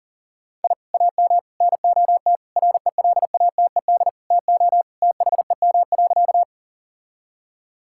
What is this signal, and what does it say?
Morse code, heard as a keyed tone, about 40 wpm, 700 Hz. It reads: I AM NOT RELATED TO THEM.
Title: Sentences from Top 1000 Words - Review Part 5 - 40wpm